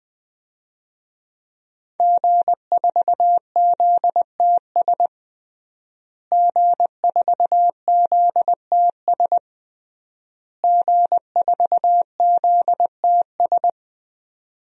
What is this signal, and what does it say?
Morse code, heard as a keyed tone, about 20 wpm, 700 Hz. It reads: G4ZTS G4ZTS G4ZTS
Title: DXCC-100 Call Signs - 5 Characters - Repeated 3x - 20wpm